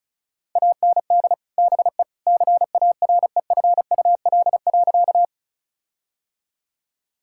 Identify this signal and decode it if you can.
Morse code, heard as a keyed tone, about 34 wpm, 700 Hz. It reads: AND BE CAREFUL.
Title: Sentences from Top 1600 Words 35wpm